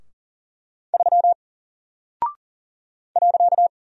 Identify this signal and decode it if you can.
Morse code, heard as a keyed tone, about 40 wpm, 700 Hz. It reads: 3 .
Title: Single Character (Letter:Number:Punctuation:Prosign) Alt 40wpm